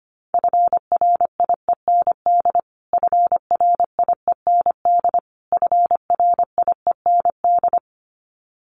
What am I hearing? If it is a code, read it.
Morse code, heard as a keyed tone, about 25 wpm, 700 Hz. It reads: FRIEND FRIEND FRIEND
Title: Top 300 Words Repeated 25wpm